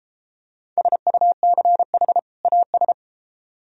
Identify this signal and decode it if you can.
Morse code, heard as a keyed tone, about 31 wpm, 700 Hz. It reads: SUCH AS